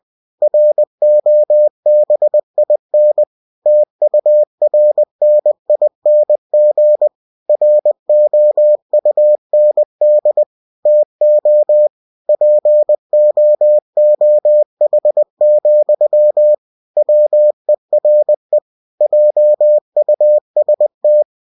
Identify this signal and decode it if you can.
Morse code, heard as a keyed tone, about 20 wpm, 600 Hz. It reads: ROBIN TURNING ROUND TO POOH, WERE JUST
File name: cw-books / winnie-the-pooh / winnie-20wpm-page-0081